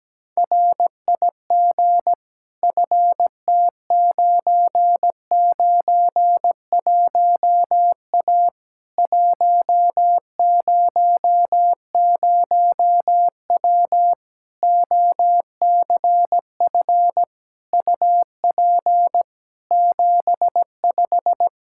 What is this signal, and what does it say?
Morse code, heard as a keyed tone, about 17 wpm, 700 Hz. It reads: RIG FT991A 100W OCF UP 75
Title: K1USN SST QSO Practice 17wpm